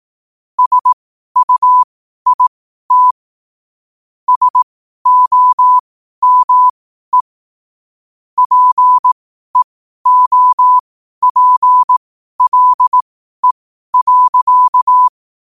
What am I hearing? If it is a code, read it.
Morse code, heard as a keyed tone, about 18 wpm, spaced as for 12 wpm, 1000 Hz. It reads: SUIT SOME PEOPLE.
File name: qotdfiles / 12wpm